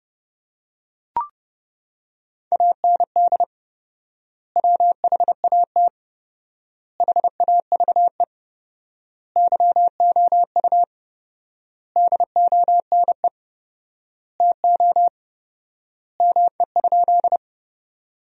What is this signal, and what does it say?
Morse code, heard as a keyed tone, about 30 wpm, 700 Hz. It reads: AND WHAT HAVE YOU DONE TO ME?
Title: 4x Word-Spacing - Sentences from Top 500 Words - 30wpm